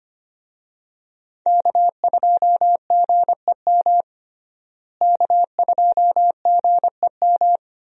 Morse code, heard as a keyed tone, about 25 wpm, 700 Hz. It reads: K2GEM K2GEM
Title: USA Call Signs - 5 Characters - Repeated 2x - 25wpm